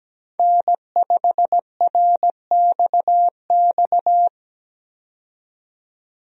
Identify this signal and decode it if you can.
Morse code, heard as a keyed tone, about 17 wpm, 700 Hz. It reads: N5RXX
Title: USA Call Signs - 5 Characters - 17wpm